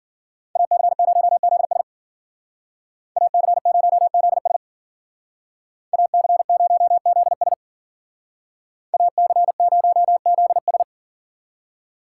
Morse code, heard as a keyed tone, about 53 wpm, 700 Hz. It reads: AC0ZS AC0ZS AC0ZS AC0ZS